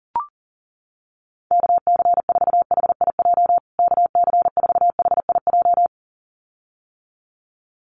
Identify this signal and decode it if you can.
Morse code, heard as a keyed tone, about 40 wpm, 700 Hz. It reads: KC4HIJ KC4HIJ
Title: USA Call Signs Repeated 2x 40wpm